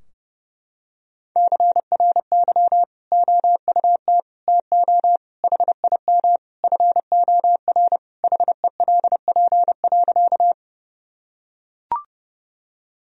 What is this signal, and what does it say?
Morse code, heard as a keyed tone, about 30 wpm, 700 Hz. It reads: CRY OUT TO HIM FOR HELP.
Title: Sentences from Top 1000 Words - Review Part 2 - 30wpm